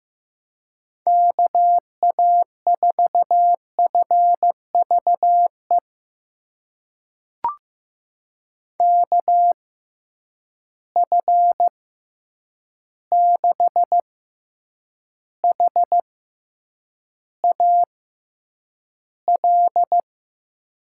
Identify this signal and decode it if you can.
Morse code, heard as a keyed tone, about 15 wpm, spaced as for 8 wpm, 700 Hz. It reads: KA4FVE KF6HAL
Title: USA Call Signs - 6 Characters - 6x Character Spacing - 15wpm